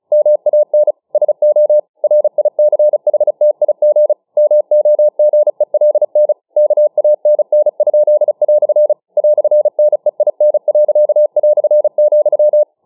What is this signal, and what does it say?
Morse code, heard as a keyed tone, about 35 wpm, 600 Hz. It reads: MAN SO RICHTIG MOGELN KANN?" "NEIN.",